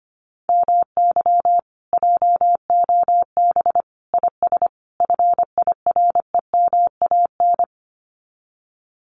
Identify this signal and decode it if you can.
Morse code, heard as a keyed tone, about 25 wpm, 700 Hz. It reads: MY JOB IS FIREMAN
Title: Common QSO Phrases Encore 25wpm